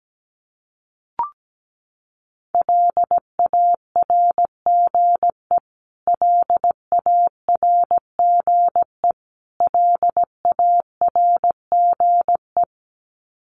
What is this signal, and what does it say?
Morse code, heard as a keyed tone, about 17 wpm, 700 Hz. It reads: LARGE LARGE LARGE
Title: Top 500 Words Repeated 17wpm